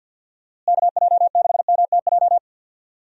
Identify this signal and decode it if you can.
Morse code, heard as a keyed tone, about 50 wpm, 700 Hz. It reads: KJ6MTJ
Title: USA Call Signs - 6 Characters - Repeated 3x - 50wpm